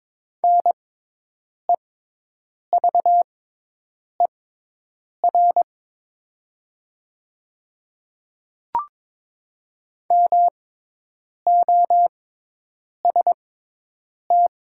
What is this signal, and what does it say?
Morse code, heard as a keyed tone, about 22 wpm, spaced as for 8 wpm, 700 Hz. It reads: NEVER MOST
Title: 6x Character Spacing - Top 300 Words - 22wpm